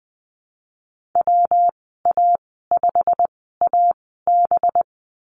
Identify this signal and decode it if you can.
Morse code, heard as a keyed tone, about 20 wpm, spaced as for 15 wpm, 700 Hz. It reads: WA5AB
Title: 219 Lesson 35g - Number 0 - Call Signs 20wpm